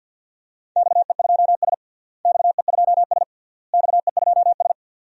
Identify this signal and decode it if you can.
Morse code, heard as a keyed tone, about 48 wpm, 700 Hz. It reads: XE2S XE2S XE2S
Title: DXCC-100 Call Signs - 4 Characters - Repeated 3x - 50wpm